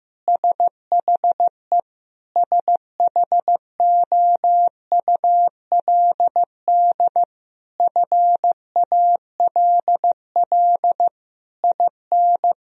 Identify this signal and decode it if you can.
Morse code, heard as a keyed tone, about 15 wpm, 700 Hz. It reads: SHE SHOULD FALL IN
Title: Sentences from Top 1000 Words - Review Part 4 - 15wpm